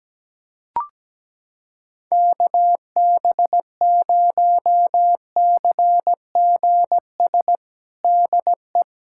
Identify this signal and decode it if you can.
Morse code, heard as a keyed tone, about 17 wpm, 700 Hz. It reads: KB0CGS DE
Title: USA Callsign DE Callsign - 6 Characters - 17wpm